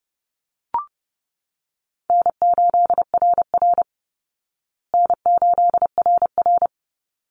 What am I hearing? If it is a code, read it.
Morse code, heard as a keyed tone, about 30 wpm, 700 Hz. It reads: N8RR N8RR
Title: Easy Call Signs Repeated 2x - 30wpm